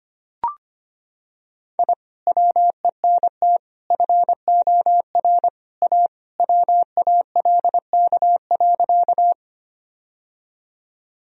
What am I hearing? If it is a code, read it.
Morse code, heard as a keyed tone, about 25 wpm, 700 Hz. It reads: I WENT FOR A WALK.
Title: Sentences from Top 1000 Words - Review Part 1 - 25wpm